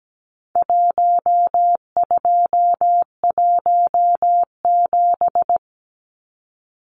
Morse code, heard as a keyed tone, about 17 wpm, 700 Hz. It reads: 1217
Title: Numbers - Contest Style - 4-Digit 17wpm